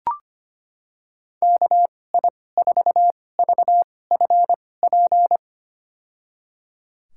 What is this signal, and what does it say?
Morse code, heard as a keyed tone, about 25 wpm, spaced as for 18 wpm, 700 Hz. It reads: KI4VFP
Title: USA Call Signs - 2x Character Spacing - 25wpm